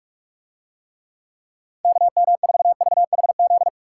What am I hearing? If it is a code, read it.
Morse code, heard as a keyed tone, about 46 wpm, 700 Hz. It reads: KM4VHZ